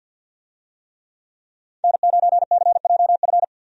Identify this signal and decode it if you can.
Morse code, heard as a keyed tone, about 51 wpm, 700 Hz. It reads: N9CJF
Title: USA Call Signs - 5 Characters - 50wpm